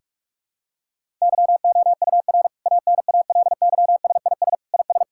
Morse code, heard as a keyed tone, about 42 wpm, 700 Hz. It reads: YOUR ANALYSIS IS
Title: Sentences from Top 1000 Words - Review Part 5 - 45wpm